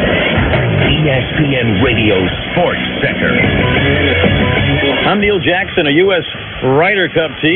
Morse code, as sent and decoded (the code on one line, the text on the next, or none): none